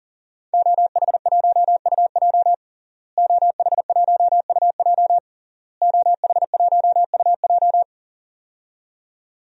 Morse code, heard as a keyed tone, about 40 wpm, 700 Hz. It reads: OH1UJ OH1UJ OH1UJ
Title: DXCC-100 Call Signs - 5 Characters - Repeated 3x - 40wpm